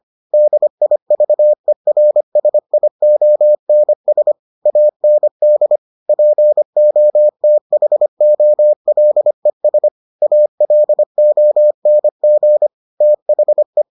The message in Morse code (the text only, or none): DIVERSIONS AND POTHOLES ALONG THE